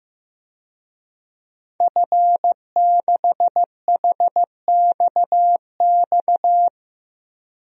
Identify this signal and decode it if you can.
Morse code, heard as a keyed tone, about 15 wpm, 700 Hz. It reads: F6HXX
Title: DXCC-100 Call Signs - 5 Characters - 15wpm